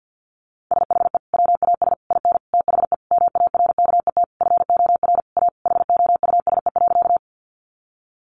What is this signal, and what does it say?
Morse code, heard as a keyed tone, about 50 wpm, 700 Hz. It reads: SHE WAS IN THE MARKET FOR A HOUSE.
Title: Sentences from Top 600 Words 50wpm